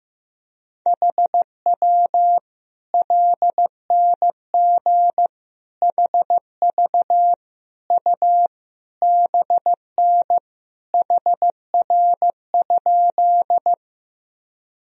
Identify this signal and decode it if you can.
Morse code, heard as a keyed tone, about 16 wpm, 700 Hz. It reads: HW LNG HV U BN HR?
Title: Sentences from Top 200 Words and Abbreviations 15wpm